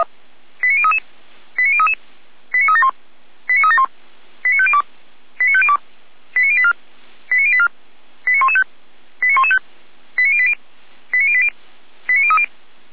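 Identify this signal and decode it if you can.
Five-tone selective call (ZVEI), heard as quick runs of five tones, tones of about 80 ms. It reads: 89033 89033 89371 89371 89502 89502 89084 89084 89115 89115 89080 89080 89030
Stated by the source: ZVEI five-tone selective call 89094 received via SDR.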